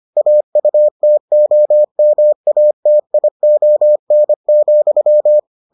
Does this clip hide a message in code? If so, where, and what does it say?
Yes: Morse code from 0.2 to 5.4 s, heard as a keyed tone, about 25 wpm, 600 Hz. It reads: AUTOMATION,